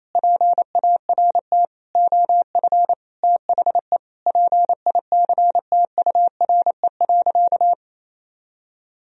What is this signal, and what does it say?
Morse code, heard as a keyed tone, about 28 wpm, 700 Hz. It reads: PART OF THE PICTURE.
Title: Sentences from Top 1000 Words - Review Part 4 - 28wpm